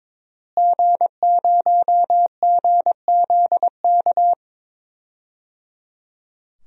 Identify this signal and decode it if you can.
Morse code, heard as a keyed tone, about 22 wpm, 700 Hz. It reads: G0GZK
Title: DXCC-100 Call Signs - 5 Characters - Repeated 3x - 22wpm